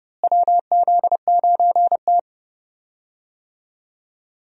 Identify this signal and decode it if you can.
Morse code, heard as a keyed tone, about 30 wpm, 700 Hz. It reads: WZ9T
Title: USA Call Signs - 4 Characters - Repeated 2x - 30wpm